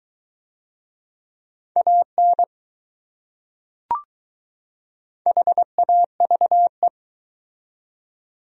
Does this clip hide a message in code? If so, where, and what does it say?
Yes: Morse code from 1.8 to 6.9 s, heard as a keyed tone, about 21 wpm, 700 Hz. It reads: AN HAVE